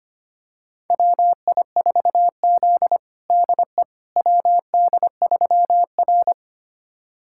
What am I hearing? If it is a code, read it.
Morse code, heard as a keyed tone, about 25 wpm, 700 Hz. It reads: WI4Z DE WD3R
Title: USA Callsign DE Callsign - 4 Characters - 25wpm